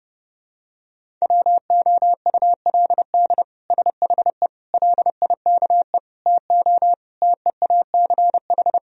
Morse code, heard as a keyed tone, about 28 wpm, 700 Hz. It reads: WOULD SHE LIKE TO TEACH
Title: Sentences from Top 1000 Words - Review Part 2 - 30wpm